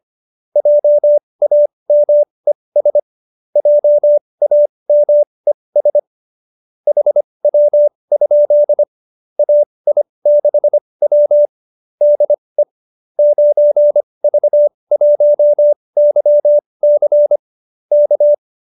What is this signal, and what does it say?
Morse code, heard as a keyed tone, about 25 wpm, spaced as for 20 wpm, 590 Hz. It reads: JAMES JAMES HW? AI6W DE 9V1YC K
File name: qso207b 20